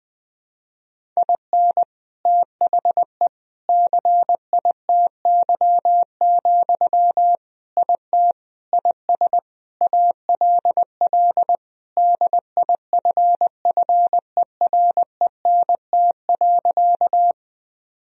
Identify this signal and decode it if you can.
Morse code, heard as a keyed tone, about 19 wpm, 700 Hz. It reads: IN THE CITY, IT IS ALL DIFFERENT.
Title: Sentences from Top 1000 Words - Review Part 3 - 20wpm